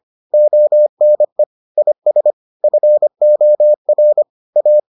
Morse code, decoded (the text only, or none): ONE IS FOR A